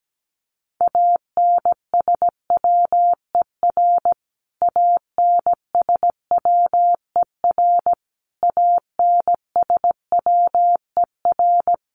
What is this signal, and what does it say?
Morse code, heard as a keyed tone, about 17 wpm, 700 Hz. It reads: ANSWER ANSWER ANSWER
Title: Top 500 Words Repeated 17wpm